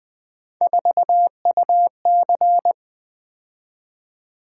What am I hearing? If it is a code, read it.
Morse code, heard as a keyed tone, about 20 wpm, 700 Hz. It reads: 4UC
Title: Three Letter-Number Combinations - 20wpm